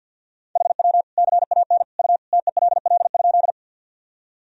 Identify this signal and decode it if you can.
Morse code, heard as a keyed tone, about 52 wpm, 700 Hz. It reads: HW CAN U TELL?
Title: Sentences from Top 200 Words and Abbreviations 50wpm